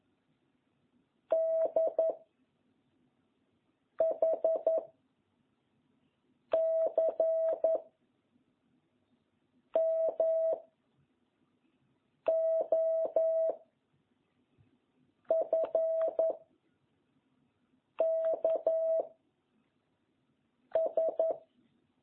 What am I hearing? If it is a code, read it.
Morse code, heard as a keyed tone, about 11 wpm, 650 Hz. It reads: D H C M O F K S